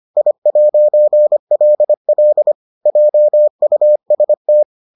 Morse code, heard as a keyed tone, about 25 wpm, 600 Hz. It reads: I'LL JUST